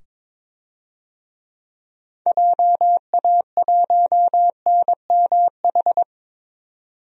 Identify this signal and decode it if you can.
Morse code, heard as a keyed tone, about 22 wpm, 700 Hz. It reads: JA1NMH
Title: DXCC-100 Call Signs - 6 Characters - 22wpm